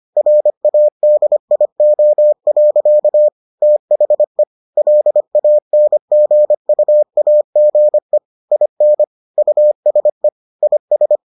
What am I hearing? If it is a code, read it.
Morse code, heard as a keyed tone, about 25 wpm, 600 Hz. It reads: RADIO. THE LANGUAGE IN USE IS